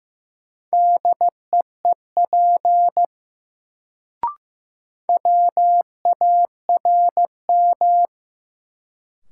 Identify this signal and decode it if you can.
Morse code, heard as a keyed tone, about 15 wpm, 700 Hz. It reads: DEEP WARM
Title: Sentences from Top 500 Words - New Word Drill - 15wpm